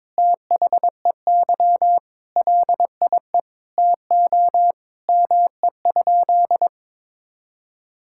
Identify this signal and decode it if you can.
Morse code, heard as a keyed tone, about 22 wpm, 700 Hz. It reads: THEY LIE TO ME?
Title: Sentences from Top 1600 Words 22wpm